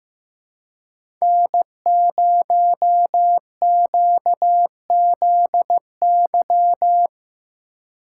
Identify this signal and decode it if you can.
Morse code, heard as a keyed tone, about 15 wpm, 700 Hz. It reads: N0QZY
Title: USA Call Signs - 3x Character Spacing - 15wpm